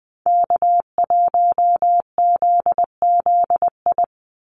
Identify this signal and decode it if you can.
Morse code, heard as a keyed tone, about 20 wpm, 700 Hz. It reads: K1ZZI
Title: Call Signs - 20wpm